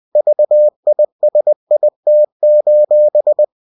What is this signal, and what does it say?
Morse code, heard as a keyed tone, about 20 wpm, 600 Hz. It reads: VISIT:
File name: cw-books / winnie-the-pooh / winnie-20wpm-page-0122